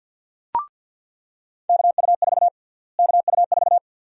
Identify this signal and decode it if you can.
Morse code, heard as a keyed tone, about 50 wpm, 700 Hz. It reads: KU4 KU4
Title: USA Call Sign Prefixes - Repeated 2x - 50wpm